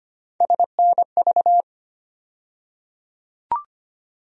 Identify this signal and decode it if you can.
Morse code, heard as a keyed tone, about 25 wpm, 700 Hz. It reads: SNV
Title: USA Call Sign Postfixes - 3 Letters 25wpm